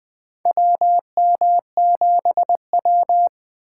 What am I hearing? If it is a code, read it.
Morse code, heard as a keyed tone, about 20 wpm, 700 Hz. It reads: WM7W
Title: USA Call Signs - 6x Character Spacing - 20wpm